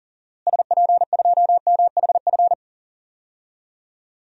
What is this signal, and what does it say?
Morse code, heard as a keyed tone, about 37 wpm, 700 Hz. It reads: SP2MHF
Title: DXCC-100 Call Signs - 6 Characters - Repeated 2x - 40wpm